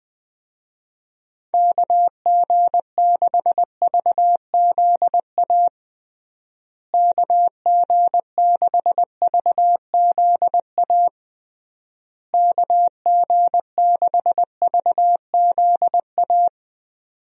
Morse code, heard as a keyed tone, about 20 wpm, 700 Hz. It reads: KG6VZA KG6VZA KG6VZA
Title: USA Call Signs - 6 Characters - Repeated 3x - 20wpm